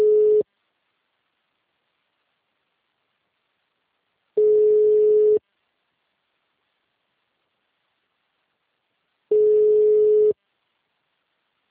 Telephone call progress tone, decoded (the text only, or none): ringback tone